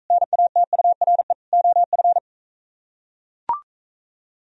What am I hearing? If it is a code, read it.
Morse code, heard as a keyed tone, about 44 wpm, 700 Hz. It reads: NATURE OF